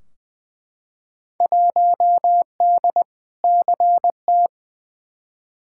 Morse code, heard as a keyed tone, about 20 wpm, 700 Hz. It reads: 1D CT
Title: Field Day Exchanges - 20wpm